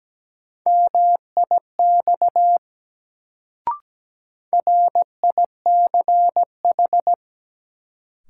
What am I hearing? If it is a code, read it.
Morse code, heard as a keyed tone, about 17 wpm, 700 Hz. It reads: MIX RICH
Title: Sentences from Top 1100 Words - New Word Drill Alt - 17wpm